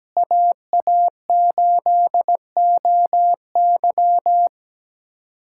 Morse code, heard as a keyed tone, about 17 wpm, 700 Hz. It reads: AA8OY
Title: K1USN SST QSO Practice - 17wpm